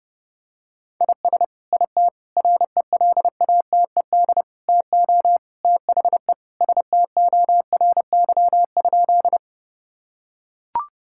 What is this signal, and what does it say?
Morse code, heard as a keyed tone, about 30 wpm, 700 Hz. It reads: IS IT RELATED TO THE STORY?
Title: Sentences from Top 1000 Words - Review Part 5 - 30wpm